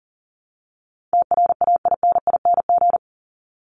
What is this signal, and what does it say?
Morse code, heard as a keyed tone, about 40 wpm, 700 Hz. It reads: TRAINING
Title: Top 501-750 Words 40wpm